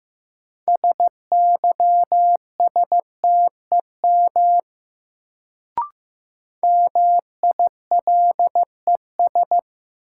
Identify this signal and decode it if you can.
Morse code, heard as a keyed tone, about 15 wpm, 700 Hz. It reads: SYSTEM MILES
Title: Sentences from Top 1000 Words - Review Part 3 - New Word Drill - 15wpm